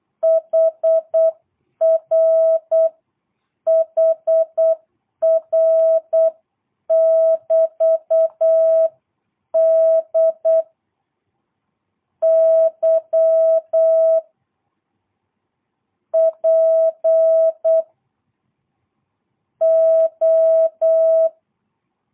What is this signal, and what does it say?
Morse code, heard as a keyed tone, about 8 wpm, 660 Hz. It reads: HR HR <BT> D Y P O
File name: jh8cbh / 2022cw